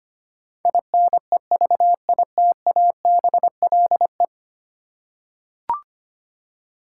Morse code, heard as a keyed tone, about 25 wpm, 700 Hz. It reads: INEVITABLE
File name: 10 Letter Words - First 3 Letters Given - 25wpm